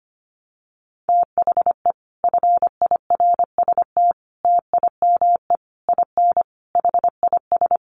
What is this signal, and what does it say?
Morse code, heard as a keyed tone, about 25 wpm, 700 Hz. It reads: THE FIRST TIME IN HIS
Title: Sets of 5 Words Encore2 25wpm